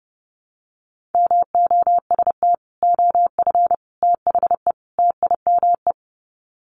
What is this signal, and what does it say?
Morse code, heard as a keyed tone, about 30 wpm, 700 Hz. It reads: MOST OF THE TIME
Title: Sets of 4 Words 30wpm